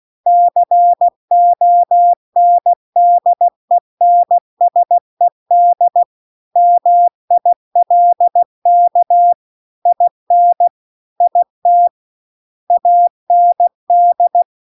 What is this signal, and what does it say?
Morse code, heard as a keyed tone, about 16 wpm, 700 Hz. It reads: CONDENSED MILK IN IT AND